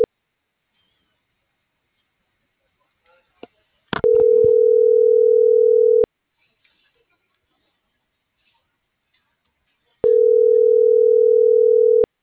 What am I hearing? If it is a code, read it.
Ringback tone (North America), 440 + 480 Hz.